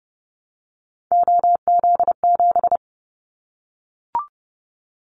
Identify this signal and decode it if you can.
Morse code, heard as a keyed tone, about 30 wpm, 700 Hz. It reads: OZ7
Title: Callsign Prefixes for Top 100 DXCC - 30wpm